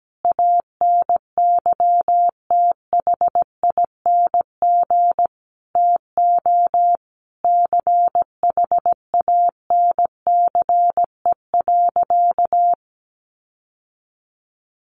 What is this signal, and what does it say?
Morse code, heard as a keyed tone, about 17 wpm, 700 Hz. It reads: ANYTHING TO CHANCE.
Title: Sentences from Top 1100 Words 17wpm